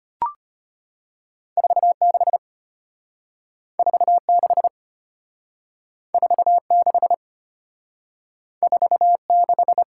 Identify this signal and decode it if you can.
Morse code, heard as a keyed tone, about 35 wpm, 700 Hz. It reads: 46 46 46 46